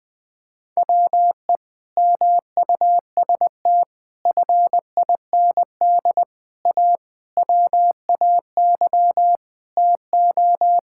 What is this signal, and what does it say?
Morse code, heard as a keyed tone, about 21 wpm, 700 Hz. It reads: WE MUST FIND A WAY TO
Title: Sentences with 4-Word Phrases - 20wpm